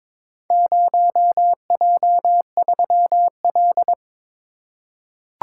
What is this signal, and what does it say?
Morse code, heard as a keyed tone, about 23 wpm, 700 Hz. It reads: 0J3L
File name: practice / 22wpm / Four Letter-Number Combinations - 22wpm